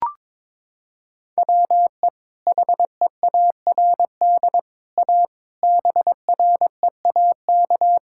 Morse code, heard as a keyed tone, about 22 wpm, 700 Hz. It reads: WE HEARD A BREAK
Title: Sentences from Top 1000 Words - Review Part 6 - 22wpm